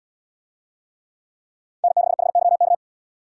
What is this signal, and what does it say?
Morse code, heard as a keyed tone, about 68 wpm, 700 Hz. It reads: N6HYK